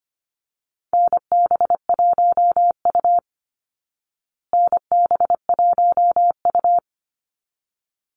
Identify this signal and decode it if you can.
Morse code, heard as a keyed tone, about 25 wpm, 700 Hz. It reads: NB1U NB1U
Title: Call Signs Repeated 3x - 25wpm